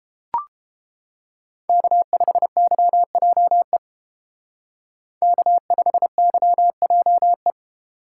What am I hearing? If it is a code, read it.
Morse code, heard as a keyed tone, about 31 wpm, 700 Hz. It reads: K5YJE K5YJE